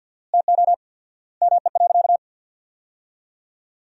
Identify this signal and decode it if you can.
Morse code, heard as a keyed tone, about 50 wpm, 700 Hz. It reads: TO ME.
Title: 4x Word-Spacing - Sentences from Top 500 Words - 50wpm